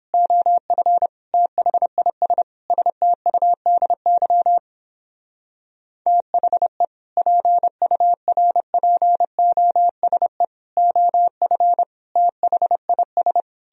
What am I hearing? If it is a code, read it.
Morse code, heard as a keyed tone, about 29 wpm, 700 Hz. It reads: OF THIS STUDY THE PURPOSE OF THIS